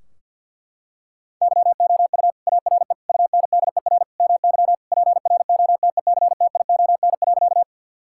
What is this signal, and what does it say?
Morse code, heard as a keyed tone, about 50 wpm, 700 Hz. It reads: YOU ARE UNDER MY PROTECTION.
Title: Sentences from Top 1600 Words 50wpm